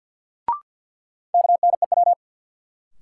Morse code, heard as a keyed tone, about 52 wpm, 700 Hz. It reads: KNEW
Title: Sentences from Top 400 Words - New Word Drill - 50wpm